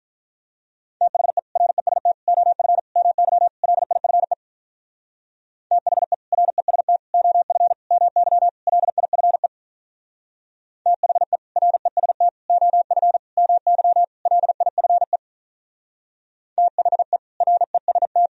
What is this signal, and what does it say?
Morse code, heard as a keyed tone, about 44 wpm, 700 Hz. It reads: THE REST OF MY LIFE THE REST OF MY LIFE THE REST OF MY LIFE THE REST
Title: Speed-Racing - Sets of 5 Words 35wpm